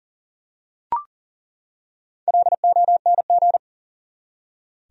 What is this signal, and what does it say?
Morse code, heard as a keyed tone, about 40 wpm, 700 Hz. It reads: LONG
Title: Top 100 Words 40wpm